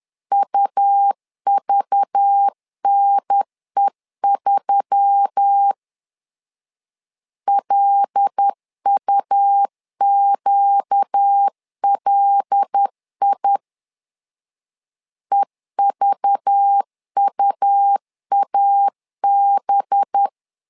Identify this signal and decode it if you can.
Morse code, heard as a keyed tone, about 10 wpm, 800 Hz. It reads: UVNE3 LUQLI EVUAB
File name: LEZIONI / 5 LBVC / 5 LBV 10wpm 2248